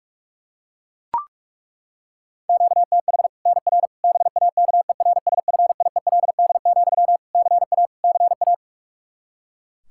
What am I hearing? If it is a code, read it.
Morse code, heard as a keyed tone, about 41 wpm, 700 Hz. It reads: QTH NR BAKERSFIELD, CA CA
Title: Practice QSO - Exchange 1 - Beginning - 45wpm